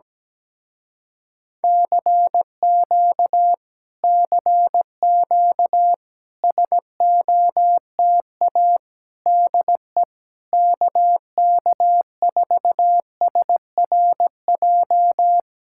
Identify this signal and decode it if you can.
Morse code, heard as a keyed tone, about 17 wpm, 700 Hz. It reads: CQ CQ SOTA DE KK4SRJ
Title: Speed-Racing Common QSO Phrases - 17wpm